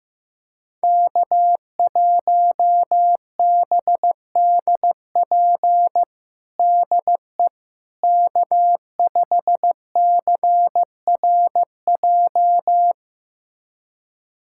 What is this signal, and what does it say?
Morse code, heard as a keyed tone, about 15 wpm, 700 Hz. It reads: K1BDP DE K5CRJ
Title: USA Callsign DE Callsign - 5 Characters - 15wpm